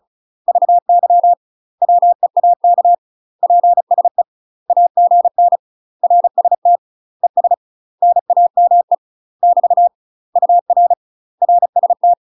Morse code, heard as a keyed tone, about 34 wpm, 700 Hz. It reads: VY WEAK PSE AGN RST ES NAME <BT> UR RST
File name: UK MASTER QSOs TRAINING / 35WPM / UK MASTER QSOs 0024